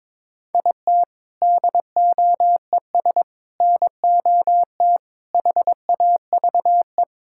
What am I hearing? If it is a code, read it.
Morse code, heard as a keyed tone, about 22 wpm, 700 Hz. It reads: IT DOES NOT HAVE
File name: practice / 22wpm / Build A Phrase - 5-Word Phrases - 22wpm